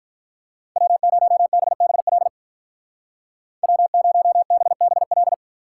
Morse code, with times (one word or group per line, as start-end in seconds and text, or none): 0.8-2.3 W0BBL
3.6-5.3 W0BBL